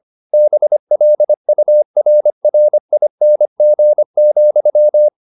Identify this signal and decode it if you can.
Morse code, heard as a keyed tone, about 25 wpm, 600 Hz. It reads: BLURRING,
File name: the old man and the sea part 1 25wpm